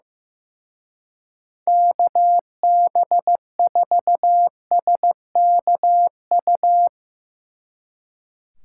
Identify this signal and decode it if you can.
Morse code, heard as a keyed tone, about 15 wpm, 700 Hz. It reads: KB4SKU
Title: Speed Racing - USA 2x3 Call Signs 15wpm